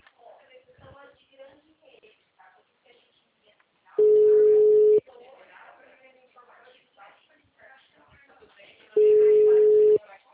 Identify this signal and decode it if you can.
Ringback tone (Europe, 425 Hz), with speech.